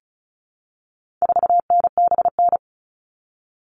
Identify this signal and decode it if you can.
Morse code, heard as a keyed tone, about 34 wpm, 700 Hz. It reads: 4NBN
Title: All Cut Numbers - 4-Digit 35wpm